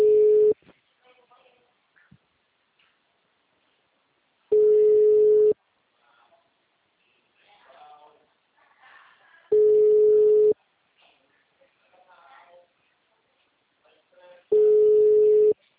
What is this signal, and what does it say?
Ringback tone (Europe, 425 Hz), with speech.